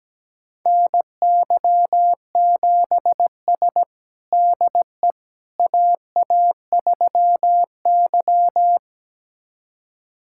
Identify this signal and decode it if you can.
Morse code, heard as a keyed tone, about 17 wpm, 700 Hz. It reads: NY7S DE AA3Y
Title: USA Callsign DE Callsign - 4 Characters - 17wpm